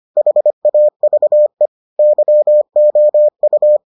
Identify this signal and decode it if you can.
Morse code, heard as a keyed tone, about 25 wpm, 600 Hz. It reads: HAVE YOU